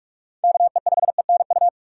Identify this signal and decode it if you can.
Morse code, heard as a keyed tone, about 44 wpm, 690 Hz. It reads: KE5ENU
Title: USA Call Signs - 6x Character Spacing - 45wpm